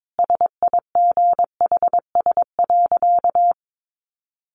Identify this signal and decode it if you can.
Morse code, heard as a keyed tone, about 22 wpm, 700 Hz. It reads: SIGHS.
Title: A Sound Of Thunder 22wpm